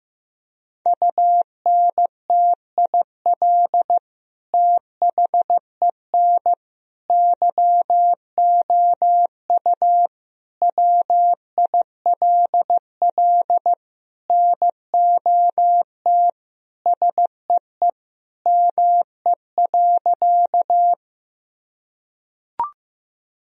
Morse code, heard as a keyed tone, about 15 wpm, 700 Hz. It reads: UNTIL THEN YOU WILL NOT SEE ME.
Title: Sentences from Top 1000 Words - Review Part 1 - 15wpm